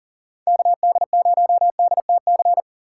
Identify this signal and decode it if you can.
Morse code, heard as a keyed tone, about 40 wpm, 700 Hz. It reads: KD0DTC
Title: USA Call Signs - 6 Characters - 4x Character Spacing - 40wpm